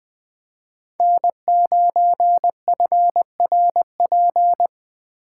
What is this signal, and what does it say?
Morse code, heard as a keyed tone, about 24 wpm, 700 Hz. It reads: N9FRP
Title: Speed Racing - USA 1x3 Call Signs 20wpm